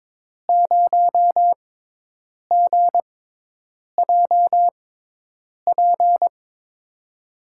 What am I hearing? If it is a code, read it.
Morse code, heard as a keyed tone, about 22 wpm, spaced as for 8 wpm, 700 Hz. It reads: 0GJP
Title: USA Call Signs - 5 Characters - 6x Character Spacing - 22wpm